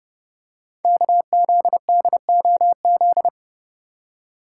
Morse code, heard as a keyed tone, about 30 wpm, 700 Hz. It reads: KZDOZ
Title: Code Groups - 5 Letter:Numbers - 30wpm